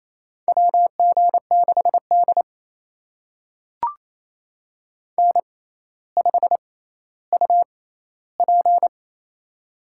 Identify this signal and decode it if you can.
Morse code, heard as a keyed tone, about 27 wpm, spaced as for 14 wpm, 700 Hz. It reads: WG6D N5UP
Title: USA Call Signs - 4 Characters - 6x Character Spacing - 28wpm